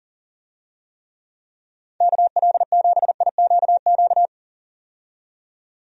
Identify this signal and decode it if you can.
Morse code, heard as a keyed tone, about 38 wpm, 700 Hz. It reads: KL7IQQ
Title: DXCC-100 Call Signs - 6 Characters - 40wpm